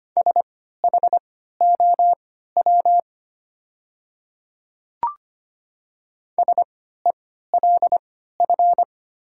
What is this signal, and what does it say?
Morse code, heard as a keyed tone, about 25 wpm, spaced as for 14 wpm, 700 Hz. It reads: SHOW SELF
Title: 3x Character Spacing - Top 500 Words - 25wpm